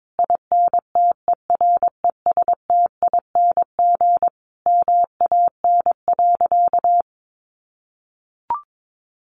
Morse code, heard as a keyed tone, about 22 wpm, 700 Hz. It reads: INTERESTING MAN.
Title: Sentences from Top 1000 Words 22wpm